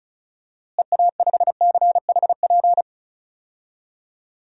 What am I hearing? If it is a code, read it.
Morse code, heard as a keyed tone, about 36 wpm, 700 Hz. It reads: EA5CHP
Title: DXCC-100 Call Signs - 6 Characters - Repeated 2x - 35wpm